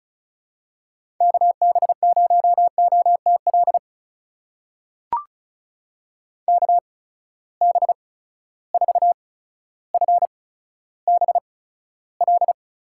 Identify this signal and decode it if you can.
Morse code, heard as a keyed tone, about 34 wpm, spaced as for 15 wpm, 700 Hz. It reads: KB0OTL KB4FBL
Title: USA Call Signs - 8x Character Spacing - 35wpm